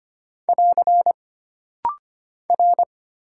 Morse code, heard as a keyed tone, about 25 wpm, 700 Hz. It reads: <AR> R
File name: Single Character (Letter:Number:Punctuation:Prosign) - Rapid-Fire - 25wpm